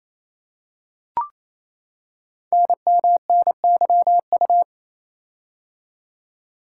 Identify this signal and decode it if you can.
Morse code, heard as a keyed tone, about 29 wpm, 700 Hz. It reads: NMNYU
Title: Code Groups - 5 Letters - 28wpm